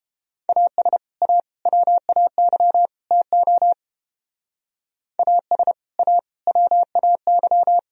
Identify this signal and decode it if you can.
Morse code, heard as a keyed tone, about 28 wpm, 700 Hz. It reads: AS A WAY TO AS A WAY